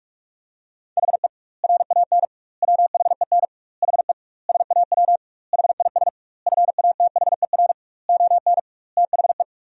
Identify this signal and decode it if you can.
Morse code, heard as a keyed tone, about 46 wpm, 700 Hz. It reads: HE RAN WHEN HE SAW HIS FATHER ON THE
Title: 2x Word-Spacing - Sentences from Top 500 Words - 45wpm